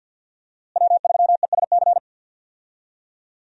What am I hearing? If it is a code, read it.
Morse code, heard as a keyed tone, about 52 wpm, 700 Hz. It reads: W3ESC
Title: USA Call Signs - 5 Characters - Repeated 2x - 50wpm